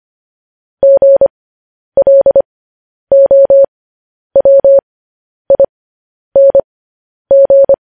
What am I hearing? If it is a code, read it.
Morse code, heard as a keyed tone, about 25 wpm, spaced as for 10 wpm, 550 Hz. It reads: GLOWING
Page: news-headlines-in-morse-code-at-10-wpm